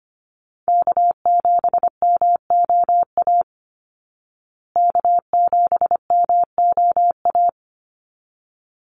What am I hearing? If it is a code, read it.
Morse code, heard as a keyed tone, about 25 wpm, 700 Hz. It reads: K7MOA K7MOA
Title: Call Signs Repeated 3x - 25wpm